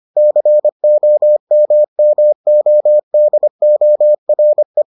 Morse code, heard as a keyed tone, about 25 wpm, 600 Hz. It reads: COMMODORE